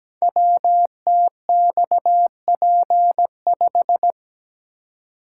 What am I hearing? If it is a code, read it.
Morse code, heard as a keyed tone, about 17 wpm, 690 Hz. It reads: WTXP5
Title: Code Groups - 5 Letter:Numbers - 17wpm